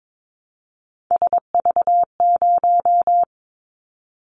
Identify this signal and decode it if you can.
Morse code, heard as a keyed tone, about 22 wpm, 700 Hz. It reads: SV0
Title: Callsign Prefixes for Top 50 DXCC - 22wpm